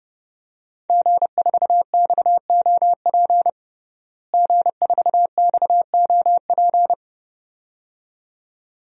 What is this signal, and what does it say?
Morse code, heard as a keyed tone, about 30 wpm, 700 Hz. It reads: G4XOP G4XOP
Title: DXCC-100 Call Signs - 5 Characters - Repeated 2x - 30wpm